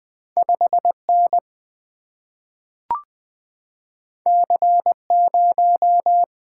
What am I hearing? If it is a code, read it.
Morse code, heard as a keyed tone, about 20 wpm, 700 Hz. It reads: 5N C0
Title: Single-Letter Single-Number 20wpm